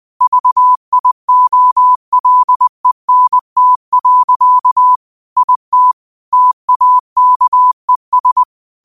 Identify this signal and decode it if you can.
Morse code, heard as a keyed tone, about 20 wpm, 1000 Hz. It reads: VIOLENT. IT TAKES
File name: qotdfiles / 20wpm